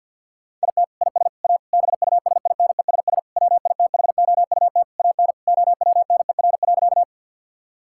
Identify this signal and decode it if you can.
Morse code, heard as a keyed tone, about 50 wpm, 700 Hz. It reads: IT IS A BUSINESS WITHOUT AN OWNER.
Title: Sentences from Top 1600 Words 50wpm